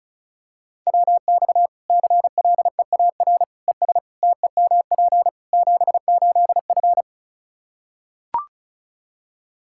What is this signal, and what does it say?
Morse code, heard as a keyed tone, about 36 wpm, 700 Hz. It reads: WX CLEAR ES TEMP 78F
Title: Practice QSO - Exchange 2 - Info - 35wpm